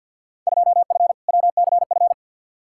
Morse code, heard as a keyed tone, about 49 wpm, 700 Hz. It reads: 2F WCF
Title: Field Day Exchanges Encore - 50wpm